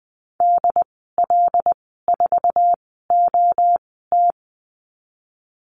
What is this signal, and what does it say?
Morse code, heard as a keyed tone, about 20 wpm, spaced as for 14 wpm, 700 Hz. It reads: DL4OT
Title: 082 Lesson 14g - Number 2 - Call Signs 20wpm